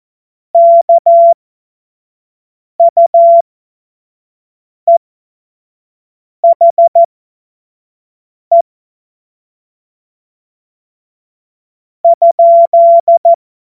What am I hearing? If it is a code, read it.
Morse code, heard as a keyed tone, about 14 wpm, spaced as for 4 wpm, 680 Hz. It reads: KUEHE?
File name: Witz-26-August-2025-25-F70